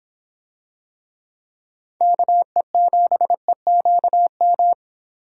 Morse code, heard as a keyed tone, about 25 wpm, 700 Hz. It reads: KE7EQM